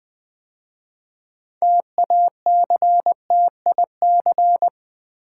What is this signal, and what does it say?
Morse code, heard as a keyed tone, about 20 wpm, 700 Hz. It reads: TACTIC
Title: 6 Letter Words - First 3 Letters Given - 20wpm